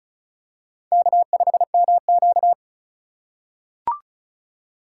Morse code, heard as a keyed tone, about 34 wpm, 700 Hz. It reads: K5MQ
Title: USA Call Signs - 2x Character Spacing - 35wpm